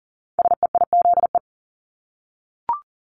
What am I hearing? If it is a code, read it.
Morse code, heard as a keyed tone, about 38 wpm, 700 Hz. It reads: SEIZE
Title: Multi-letter Phonemes Repeated 3x 40wpm